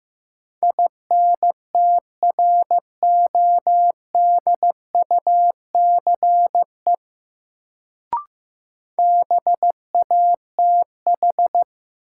Morse code, heard as a keyed tone, about 15 wpm, 700 Hz. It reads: INTRODUCE BATH
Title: Sentences from Top 1600 Words - New Word Drill - 15wpm